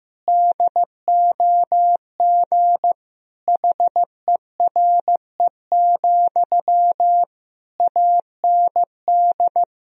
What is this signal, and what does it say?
Morse code, heard as a keyed tone, about 15 wpm, 700 Hz. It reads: DOG HERE, AND
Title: Sentences from Top 1000 Words - Review Part 2 - 15wpm